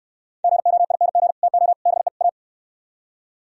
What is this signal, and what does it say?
Morse code, heard as a keyed tone, about 66 wpm, 700 Hz. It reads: GOING TO BE A